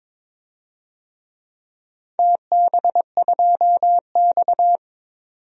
Morse code, heard as a keyed tone, about 22 wpm, 700 Hz. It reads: TB2X
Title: DXCC-100 Call Signs - 4 Characters - 22wpm